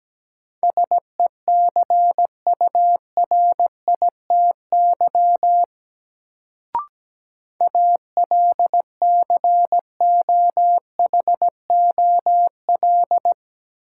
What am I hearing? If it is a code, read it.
Morse code, heard as a keyed tone, about 17 wpm, 700 Hz. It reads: SECURITY ALCOHOL
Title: Sentences from Top 1400 Words - New Word Drill Alt - 17wpm